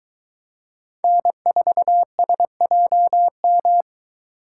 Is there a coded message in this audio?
Yes: Morse code from 1.0 to 3.8 s, heard as a keyed tone, about 23 wpm, 700 Hz. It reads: N4SJM